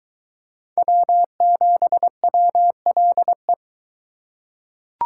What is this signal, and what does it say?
Morse code, heard as a keyed tone, about 23 wpm, 700 Hz. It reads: W7WLE